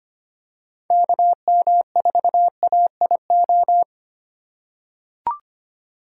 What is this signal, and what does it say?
Morse code, heard as a keyed tone, about 25 wpm, 700 Hz. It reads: KM4AIO
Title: USA Call Signs - 6 Characters - 3x Character Spacing - 25wpm